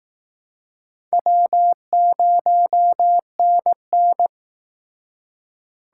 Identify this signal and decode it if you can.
Morse code, heard as a keyed tone, about 19 wpm, 700 Hz. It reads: W0NN
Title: Speed Racing - USA 1x2 Call Signs 15wpm